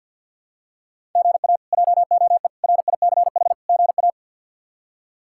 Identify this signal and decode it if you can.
Morse code, heard as a keyed tone, about 46 wpm, 700 Hz. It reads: GA JOE RICH GA
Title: K1USN SST QSO Practice - 50wpm